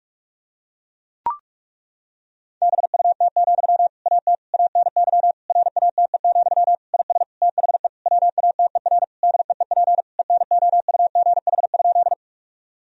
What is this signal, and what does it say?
Morse code, heard as a keyed tone, about 41 wpm, 700 Hz. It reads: BUT, AT ANY RATE, IS THE WATER DEEP ENOUGH?
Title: Sentences from Top 1000 Words - Review Part 4 - 45wpm